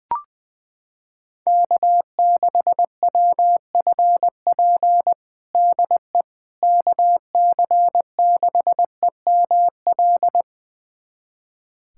Morse code, heard as a keyed tone, about 21 wpm, 700 Hz. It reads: K6WFP DE KC6EML
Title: USA Callsign DE Callsign 20wpm